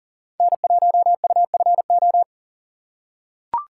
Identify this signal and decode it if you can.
Morse code, heard as a keyed tone, about 40 wpm, 700 Hz. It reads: N1UFO
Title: USA Call Signs - 5 Characters - 6x Character Spacing - 40wpm